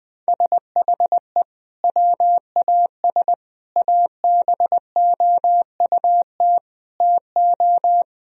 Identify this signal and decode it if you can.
Morse code, heard as a keyed tone, about 21 wpm, 700 Hz. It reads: SHE WAS ABOUT TO
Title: Sentences from Top 1000 Words - Review Part 2 - 20wpm